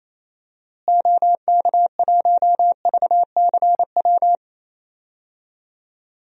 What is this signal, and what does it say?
Morse code, heard as a keyed tone, about 28 wpm, 700 Hz. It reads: OK1VCW
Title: DXCC-100 Call Signs - 6 Characters - Repeated 3x - 28wpm